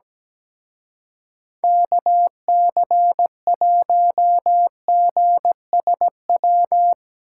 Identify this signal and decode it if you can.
Morse code, heard as a keyed tone, about 17 wpm, 700 Hz. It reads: KC1GSW
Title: Speed Racing - USA Call Signs 17wpm